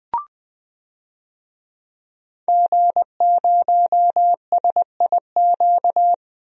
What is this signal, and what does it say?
Morse code, heard as a keyed tone, about 20 wpm, 700 Hz. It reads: G0SIQ
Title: DXCC-100 Call Signs - 5 Characters - Repeated 2x - 20wpm